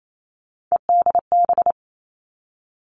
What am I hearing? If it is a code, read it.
Morse code, heard as a keyed tone, about 28 wpm, 700 Hz. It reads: EDB
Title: Three Challenging Letter-Number Combinations 28wpm